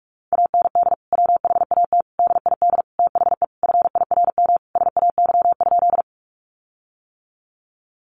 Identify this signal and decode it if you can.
Morse code, heard as a keyed tone, about 45 wpm, 700 Hz. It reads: AND WHAT DID THE FIRM SAY?
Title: Sentences from Top 1400 Words 45wpm